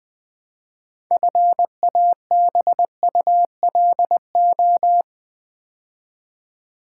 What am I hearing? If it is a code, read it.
Morse code, heard as a keyed tone, about 20 wpm, 700 Hz. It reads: FABULO
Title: Build A Word - 2 Letters At A Time - 8-Letter Words - 20wpm